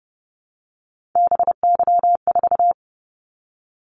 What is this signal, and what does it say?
Morse code, heard as a keyed tone, about 30 wpm, 700 Hz. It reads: BY4
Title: Callsign Prefixes for Top 100 DXCC - 30wpm